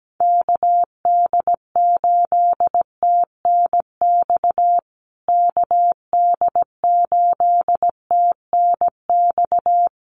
Morse code, heard as a keyed tone, about 17 wpm, 700 Hz. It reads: KD8TNX KD8TNX
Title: USA Call Signs Repeated 3x 17wpm